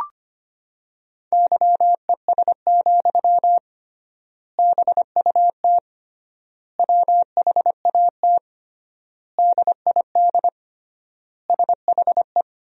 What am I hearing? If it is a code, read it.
Morse code, heard as a keyed tone, about 25 wpm, 700 Hz. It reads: YES, BUT WHAT DID SHE
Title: 3x Word-Spacing - Sentences from Top 500 Words - 25wpm